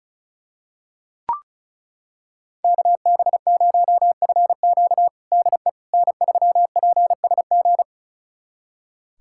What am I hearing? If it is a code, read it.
Morse code, heard as a keyed tone, about 34 wpm, 690 Hz. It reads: KB0FQ DE N3PSG
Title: USA Callsign DE Callsign - 5 Characters - 35wpm